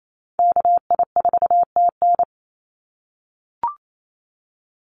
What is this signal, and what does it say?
Morse code, heard as a keyed tone, about 28 wpm, 700 Hz. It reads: KI4TN
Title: NA SOTA Chaser Callsigns 28wpm